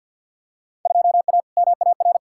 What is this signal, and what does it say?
Morse code, heard as a keyed tone, about 52 wpm, 700 Hz. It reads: 2A MAR
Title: Field Day Exchanges - 50wpm